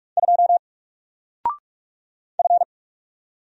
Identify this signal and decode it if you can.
Morse code, heard as a keyed tone, about 44 wpm, 700 Hz. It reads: 2 F
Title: Single Character (Letter:Number:Punctuation:Prosign) 45wpm